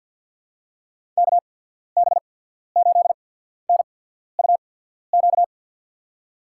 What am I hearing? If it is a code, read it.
Morse code, heard as a keyed tone, about 51 wpm, spaced as for 14 wpm, 700 Hz. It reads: KB8NUQ
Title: USA Call Signs - 6 Characters - 8x Character Spacing - 50wpm